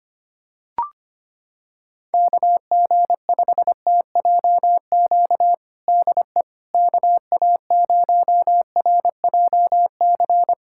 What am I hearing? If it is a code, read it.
Morse code, heard as a keyed tone, about 25 wpm, 700 Hz. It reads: KG5TJQ DE KA0RJC
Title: USA Callsign DE Callsign - 6 Characters - 25wpm